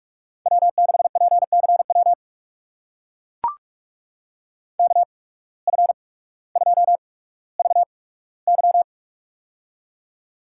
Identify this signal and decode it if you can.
Morse code, heard as a keyed tone, about 45 wpm, spaced as for 19 wpm, 700 Hz. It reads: W6PCW KF2VY
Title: USA Call Signs - 5 Characters - 8x Character Spacing - 45wpm